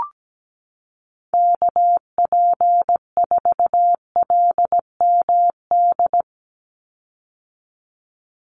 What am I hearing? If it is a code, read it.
Morse code, heard as a keyed tone, about 17 wpm, 700 Hz. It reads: KP4LMD
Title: USA Call Signs 17wpm